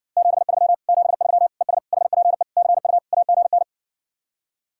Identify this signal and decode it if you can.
Morse code, heard as a keyed tone, about 60 wpm, 700 Hz. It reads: 73 73 ES HPE CU AGN